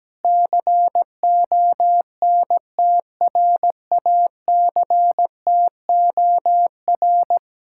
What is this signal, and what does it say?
Morse code, heard as a keyed tone, about 17 wpm, 700 Hz. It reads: CONTRACTOR
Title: Speed-Racing Common QSO Phrases - 17wpm